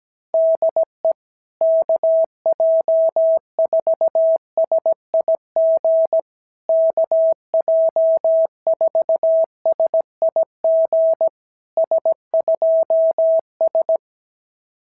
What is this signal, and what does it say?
Morse code, heard as a keyed tone, about 17 wpm, 650 Hz. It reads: DE KJ4SIG KJ4SIG S2S
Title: SOTA QSO Practice 17wpm